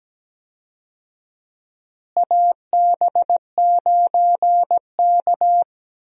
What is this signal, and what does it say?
Morse code, heard as a keyed tone, about 17 wpm, 700 Hz. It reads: AB9K
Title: USA Call Signs - 4 Characters - Repeated 2x - 17wpm